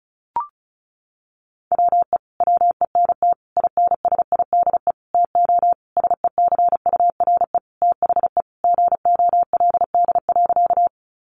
Morse code, heard as a keyed tone, about 34 wpm, 700 Hz. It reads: WE WENT INSIDE TO SECURE THE GOLD.
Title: Sentences from Top 1200 Words 35wpm